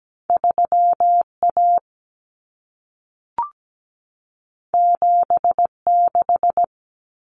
Morse code, heard as a keyed tone, about 17 wpm, 700 Hz. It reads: 3A 76
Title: Common Cut Numbers - 2-Digit 17wpm